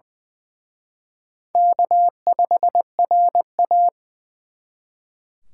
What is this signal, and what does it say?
Morse code, heard as a keyed tone, about 24 wpm, 700 Hz. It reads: K5RA
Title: Speed Racing - USA Call Signs 20wpm